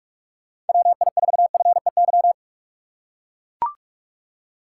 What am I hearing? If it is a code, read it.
Morse code, heard as a keyed tone, about 43 wpm, 700 Hz. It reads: WI4FEY
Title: USA Call Signs - 6 Characters - 6x Character Spacing - 45wpm